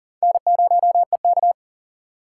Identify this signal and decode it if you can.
Morse code, heard as a keyed tone, about 37 wpm, 700 Hz. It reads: N0EK
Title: USA Call Signs - 6x Character Spacing - 40wpm